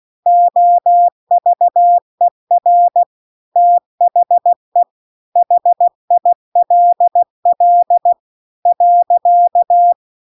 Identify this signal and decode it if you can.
Morse code, heard as a keyed tone, about 16 wpm, 700 Hz. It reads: OVER THE HILL .
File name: CW-The-Big-Twoheaded-River-2-16WPM